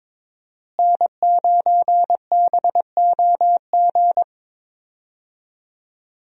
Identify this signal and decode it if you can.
Morse code, heard as a keyed tone, about 22 wpm, 700 Hz. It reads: N9BOG
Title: USA Call Signs - 5 Characters - Repeated 3x - 22wpm